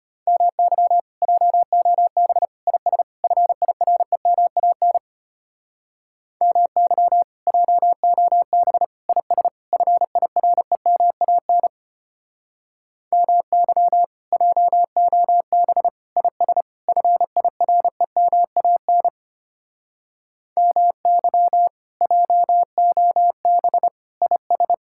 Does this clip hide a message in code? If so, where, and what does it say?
Yes: Morse code from 0.3 to 24.7 s, heard as a keyed tone, about 33 wpm, 700 Hz. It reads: MY JOB IS FIREMAN MY JOB IS FIREMAN MY JOB IS FIREMAN MY JOB IS